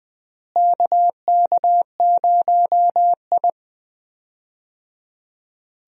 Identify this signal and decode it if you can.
Morse code, heard as a keyed tone, about 21 wpm, 700 Hz. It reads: KK0I
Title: USA Call Signs - 4 Characters - Repeated 3x - 20wpm